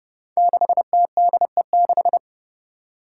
Morse code, heard as a keyed tone, about 30 wpm, 700 Hz. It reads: 6TDE6
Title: Five Challenging Letter-Number Combinations 30wpm